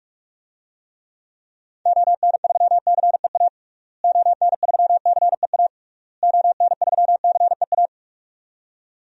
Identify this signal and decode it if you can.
Morse code, heard as a keyed tone, about 41 wpm, 700 Hz. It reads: ON3CEA ON3CEA ON3CEA
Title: DXCC-100 Call Signs - 6 Characters - Repeated 3x - 45wpm